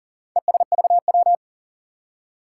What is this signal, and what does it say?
Morse code, heard as a keyed tone, about 39 wpm, 700 Hz. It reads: ESVW
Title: Four Challenging Letter-Number Combinations 40wpm